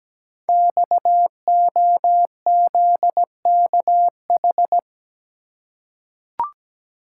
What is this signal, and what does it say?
Morse code, heard as a keyed tone, about 17 wpm, 700 Hz. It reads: XOZKH
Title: Code Groups - 5 Letter:Numbers - 17wpm